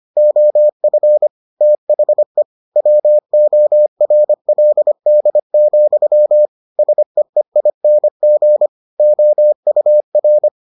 Morse code, heard as a keyed tone, about 25 wpm, 600 Hz. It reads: OF THE WORLD, SEEING OUR